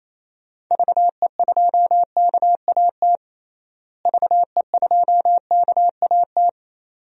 Morse code, heard as a keyed tone, about 28 wpm, 700 Hz. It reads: VE2KAT VE2KAT
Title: DXCC-100 Call Signs - 6 Characters - Repeated 2x - 28wpm